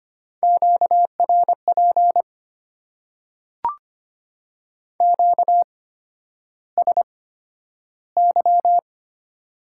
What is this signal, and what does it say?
Morse code, heard as a keyed tone, about 25 wpm, spaced as for 11 wpm, 700 Hz. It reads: QRP QSY
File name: Top 90 QSO Elements - 8x Character Spacing - 25wpm